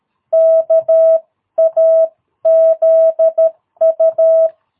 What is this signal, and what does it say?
Morse code, heard as a keyed tone, about 13 wpm, 660 Hz. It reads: KAZU
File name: jh8cbh / 2022cw